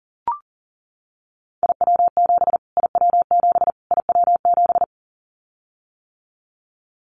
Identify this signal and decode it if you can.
Morse code, heard as a keyed tone, about 40 wpm, 700 Hz. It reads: IW7 IW7 IW7
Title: Callsign Prefixes for Top 25 DXCC - Repeated 3x - 40wpm